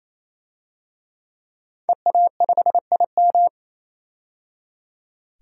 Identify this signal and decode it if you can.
Morse code, heard as a keyed tone, about 28 wpm, 700 Hz. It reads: EA5IM
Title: DXCC-100 Call Signs - 5 Characters - 28wpm